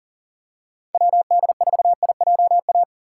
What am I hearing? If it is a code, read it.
Morse code, heard as a keyed tone, about 40 wpm, 700 Hz. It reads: WD4IJA
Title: USA Call Signs - 6 Characters - 6x Character Spacing - 40wpm